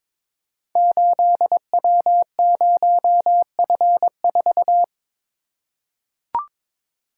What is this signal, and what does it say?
Morse code, heard as a keyed tone, about 22 wpm, 700 Hz. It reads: 8W0F4
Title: Code Groups - 5 Letter:Numbers - 22wpm